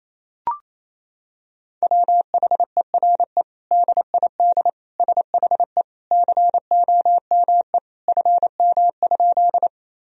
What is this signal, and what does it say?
Morse code, heard as a keyed tone, about 28 wpm, 700 Hz. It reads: WHERE DID SHE COME FM?
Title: Sentences from Top 200 Words and Abbreviations 28wpm